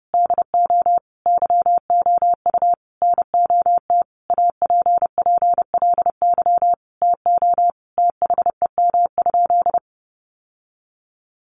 Morse code, heard as a keyed tone, about 30 wpm, 700 Hz. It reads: DO YOU NOT APPLY TO THEM?
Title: Sentences from Top 1200 Words 30wpm